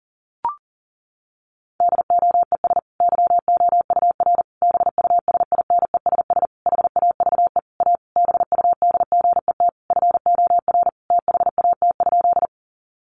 Morse code, heard as a keyed tone, about 40 wpm, 700 Hz. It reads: DOES YOUR BUSINESS HAVE A BUDGET FOR THAT?
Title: Sentences from Top 1200 Words 40wpm